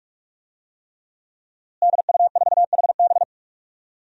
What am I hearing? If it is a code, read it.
Morse code, heard as a keyed tone, about 46 wpm, 700 Hz. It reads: DU4HB
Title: DXCC-100 Call Signs - 5 Characters - Repeated 3x - 45wpm